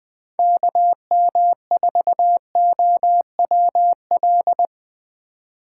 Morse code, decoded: KM4OWL

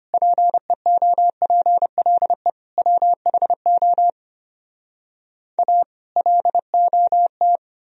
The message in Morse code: PEOPLE WHO A LOT

